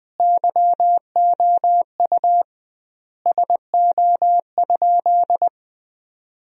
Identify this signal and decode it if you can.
Morse code, heard as a keyed tone, about 20 wpm, 700 Hz. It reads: YOU SO?
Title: 2x Word-Spacing - Sentences from Top 400 Words - 20wpm